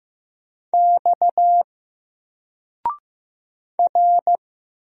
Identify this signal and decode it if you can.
Morse code, heard as a keyed tone, about 15 wpm, 700 Hz. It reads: X R
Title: Single Letter-Number 15wpm